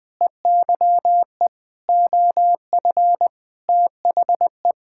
Morse code, heard as a keyed tone, about 21 wpm, 700 Hz. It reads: EYE OF THE
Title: Sentences with 8-Word Phrases - 20wpm